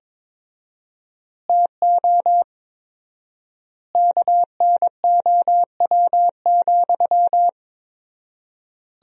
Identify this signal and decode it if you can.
Morse code, heard as a keyed tone, about 21 wpm, 700 Hz. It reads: TO KNOW,
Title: 4x Word-Spacing - Sentences from Top 200 Words - 22wpm